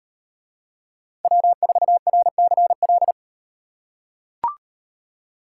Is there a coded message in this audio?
Yes: Morse code from 1.2 to 3.1 s, heard as a keyed tone, about 38 wpm, 700 Hz. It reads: W4RCL